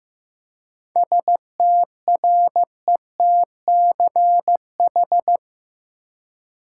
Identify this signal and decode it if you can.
Morse code, heard as a keyed tone, about 15 wpm, 700 Hz. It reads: STRETCH
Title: Seven Letter Words - 15wpm